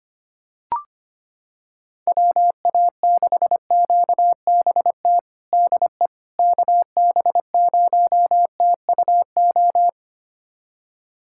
Morse code, heard as a keyed tone, about 25 wpm, 700 Hz. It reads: WA6QBT DE KB0TUO
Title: USA Callsign DE Callsign 25wpm